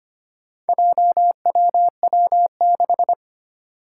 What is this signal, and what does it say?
Morse code, heard as a keyed tone, about 25 wpm, 700 Hz. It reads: JWW6
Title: Four Challenging Letter-Number Combinations 25wpm